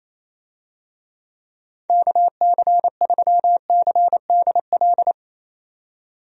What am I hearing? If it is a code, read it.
Morse code, heard as a keyed tone, about 28 wpm, 700 Hz. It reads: KC3CDL
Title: USA Call Signs - 6 Characters - 28wpm